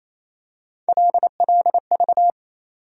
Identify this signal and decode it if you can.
Morse code, heard as a keyed tone, about 28 wpm, 700 Hz. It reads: LLV
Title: USA Call Sign Postfixes - 3 Letters 28wpm